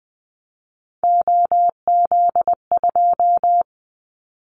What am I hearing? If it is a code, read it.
Morse code, heard as a keyed tone, about 20 wpm, 700 Hz. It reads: OZ2
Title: Callsign Prefixes for Top 50 DXCC - 20wpm